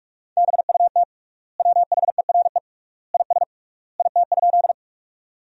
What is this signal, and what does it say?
Morse code, heard as a keyed tone, about 46 wpm, 700 Hz. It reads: BUT WHERE IS IT?
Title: 3x Word-Spacing - Sentences from Top 200 Words - 45wpm